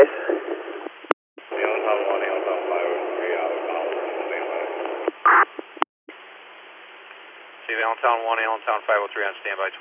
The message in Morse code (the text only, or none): none